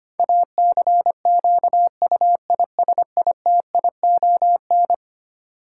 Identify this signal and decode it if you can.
Morse code, heard as a keyed tone, about 25 wpm, 700 Hz. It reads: ACQUISITION
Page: morse-code-ninja-practice-25wpm